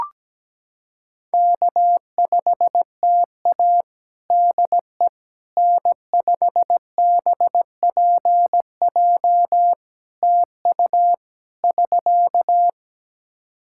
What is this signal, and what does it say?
Morse code, heard as a keyed tone, about 17 wpm, 700 Hz. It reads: K5TA DE N5BPJ TU <SK>
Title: K1USN SST QSO Practice 17wpm